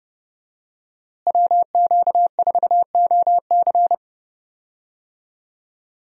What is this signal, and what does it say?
Morse code, heard as a keyed tone, about 30 wpm, 700 Hz. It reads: WQ4OC
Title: USA Call Signs - 5 Characters - Repeated 3x - 30wpm